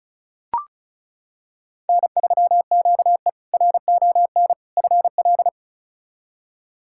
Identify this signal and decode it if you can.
Morse code, heard as a keyed tone, about 34 wpm, 700 Hz. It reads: N3QE RON FL
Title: K1USN SST QSO Practice - 35wpm